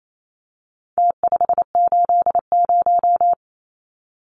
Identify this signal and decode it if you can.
Morse code, heard as a keyed tone, about 28 wpm, 700 Hz. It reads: T580
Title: Numbers - Contest Style - 4-Digit 28wpm